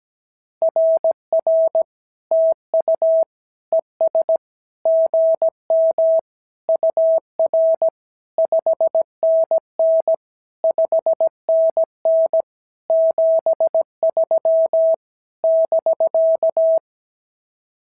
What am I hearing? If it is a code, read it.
Morse code, heard as a keyed tone, about 17 wpm, 650 Hz. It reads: RR TU ES GM UR 5NN 5NN 73 <BK>
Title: SOTA QSO Practice 17wpm